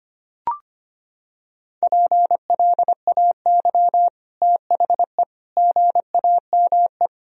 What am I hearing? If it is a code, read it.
Morse code, heard as a keyed tone, about 25 wpm, 700 Hz. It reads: PLAY THE GAME
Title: Sentences from Top 1000 Words - Review Part 3 - 25wpm